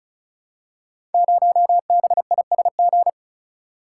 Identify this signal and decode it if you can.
Morse code, heard as a keyed tone, about 36 wpm, 700 Hz. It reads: 0BISG
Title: Code Groups - 5 Letter:Numbers - 35wpm